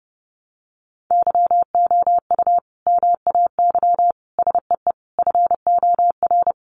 Morse code, heard as a keyed tone, about 30 wpm, 700 Hz. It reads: YOU MAY SEE FOR
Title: Sentences from Top 600 Words 30wpm